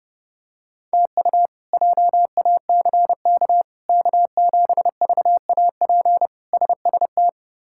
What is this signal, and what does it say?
Morse code, heard as a keyed tone, about 31 wpm, 700 Hz. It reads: TU JACK K7VAP SST
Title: K1USN SST QSO Practice - 30wpm